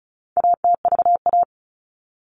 Morse code, heard as a keyed tone, about 34 wpm, 700 Hz. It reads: ATVA
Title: All Cut Numbers - 4-Digit 35wpm